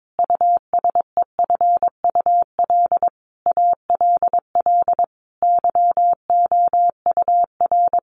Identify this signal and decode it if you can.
Morse code, heard as a keyed tone, about 22 wpm, 700 Hz. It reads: USEFUL ALL YOUR
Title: Sentences from Top 700 Words 22wpm